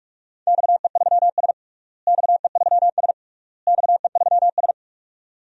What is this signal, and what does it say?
Morse code, heard as a keyed tone, about 46 wpm, 700 Hz. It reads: XE3S XE3S XE3S
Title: DXCC-100 Call Signs - 4 Characters - Repeated 3x - 45wpm